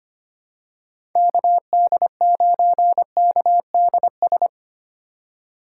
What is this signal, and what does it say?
Morse code, heard as a keyed tone, about 25 wpm, 690 Hz. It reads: KD9KDS
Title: USA Call Signs - 4x Character Spacing - 25wpm